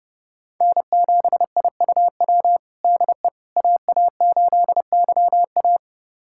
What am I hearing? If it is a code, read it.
Morse code, heard as a keyed tone, about 30 wpm, 700 Hz. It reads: N7IUW DE AA8YA
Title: USA Callsign DE Callsign - 5 Characters - 30wpm